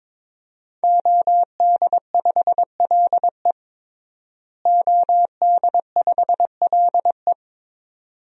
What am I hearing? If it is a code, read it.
Morse code, heard as a keyed tone, about 22 wpm, 700 Hz. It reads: OD5LE OD5LE
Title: DXCC-100 Call Signs - 5 Characters - Repeated 2x - 22wpm